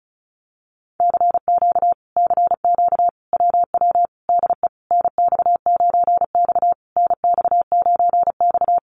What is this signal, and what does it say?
Morse code, heard as a keyed tone, about 34 wpm, 700 Hz. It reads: CQ CQ WW DE NX9X NX9X
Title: Common QSO Phrases Encore 35wpm